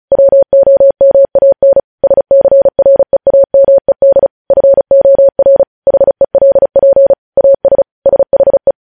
Morse code, heard as a keyed tone, about 36 wpm, 550 Hz. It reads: WOMAN SCREAMED FOR HELP AS SHE
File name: podcast / News headlines-1x-35x35